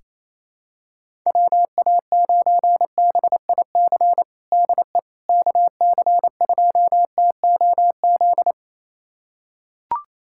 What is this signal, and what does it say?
Morse code, heard as a keyed tone, about 28 wpm, 700 Hz. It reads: WA9BIC DE KC2TOZ
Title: USA Callsign DE Callsign 28wpm